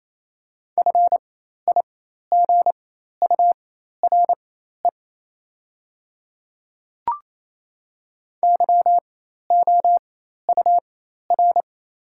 Morse code, heard as a keyed tone, about 28 wpm, spaced as for 13 wpm, 700 Hz. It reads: FIGURE YOUR
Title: 4x Character Spacing - Top 500 Words - 28wpm